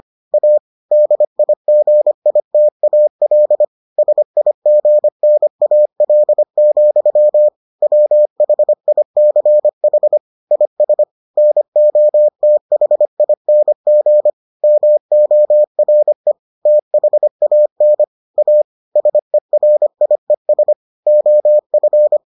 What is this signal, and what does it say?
Morse code, heard as a keyed tone, about 25 wpm, 600 Hz. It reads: A DIGITAL SIGNAL, WHICH IS NOTHING MORE THAN A SERIES OF